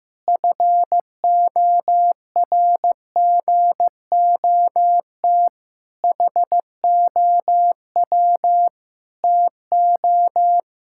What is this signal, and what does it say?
Morse code, heard as a keyed tone, about 16 wpm, 700 Hz. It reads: FORGOT HOW TO
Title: Pangram Challenge v2 - 15wpm